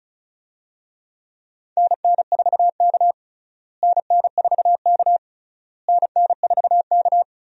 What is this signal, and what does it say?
Morse code, heard as a keyed tone, about 36 wpm, 700 Hz. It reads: NN4K NN4K NN4K
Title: USA Call Signs - 4 Characters - Repeated 3x - 35wpm